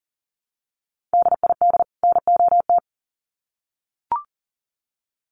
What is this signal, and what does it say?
Morse code, heard as a keyed tone, about 40 wpm, 700 Hz. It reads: DID NOT
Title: Sets of 2 Words 40wpm